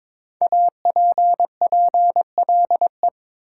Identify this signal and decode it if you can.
Morse code, heard as a keyed tone, about 22 wpm, 700 Hz. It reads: APPLE
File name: 5 Letter Words - First 3 Letters Given - 22wpm